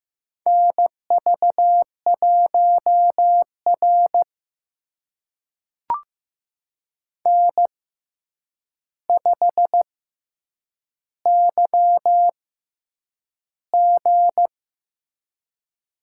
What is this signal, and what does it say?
Morse code, heard as a keyed tone, about 15 wpm, spaced as for 8 wpm, 700 Hz. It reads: NV1R N5YG
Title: USA Call Signs - 4 Characters - 6x Character Spacing - 15wpm